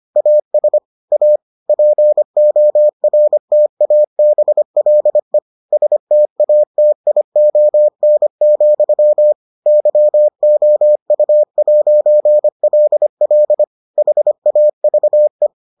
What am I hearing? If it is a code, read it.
Morse code, heard as a keyed tone, about 25 wpm, 600 Hz. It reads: AS A PORTABLE STATION, YOU'LL HAVE